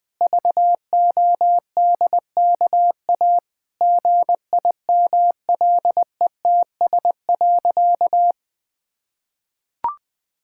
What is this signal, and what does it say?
Morse code, heard as a keyed tone, about 20 wpm, 700 Hz. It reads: VODKA GIMLETS.
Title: Pangram Challenge v2 - 20wpm